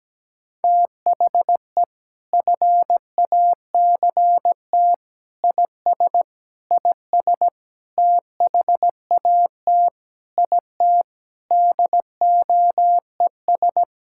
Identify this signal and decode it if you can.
Morse code, heard as a keyed tone, about 17 wpm, 700 Hz. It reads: THE FACT IS IS THAT IT DOES
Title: Sentences from Top 1600 Words 17wpm